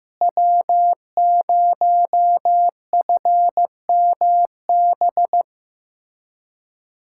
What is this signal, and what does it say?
Morse code, heard as a keyed tone, about 16 wpm, 700 Hz. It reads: W0FMB
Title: USA Callsign DE Callsign - 5 Characters - 15wpm